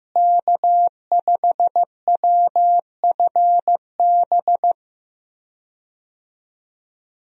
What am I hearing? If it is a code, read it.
Morse code, heard as a keyed tone, about 15 wpm, 700 Hz. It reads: K5WFB
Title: USA Call Signs - 5 Characters - Repeated 2x - 15wpm